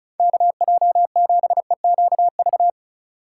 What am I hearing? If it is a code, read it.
Morse code, heard as a keyed tone, about 38 wpm, 700 Hz. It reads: KJ7EQV